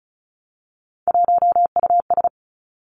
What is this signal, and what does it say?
Morse code, heard as a keyed tone, about 36 wpm, 700 Hz. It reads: 1US
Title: Three Challenging Letter-Number Combinations 35wpm